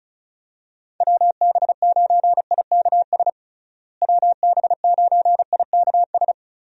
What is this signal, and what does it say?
Morse code, heard as a keyed tone, about 34 wpm, 700 Hz. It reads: WB9IKS WB9IKS
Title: USA Call Signs - 6 Characters - Repeated 2x - 35wpm